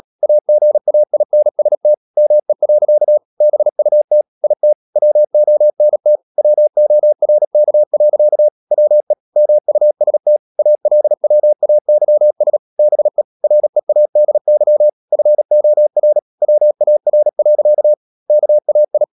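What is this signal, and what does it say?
Morse code, heard as a keyed tone, about 37 wpm, 600 Hz. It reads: AGAINST ME. BUT IT WONT WORK. WE MUST ALWAYS BE READY FOR WAR. KAI
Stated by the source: Morse code practice at 35–39 wpm sending English text of Ken Follett's novel Never, from CW med Gnister.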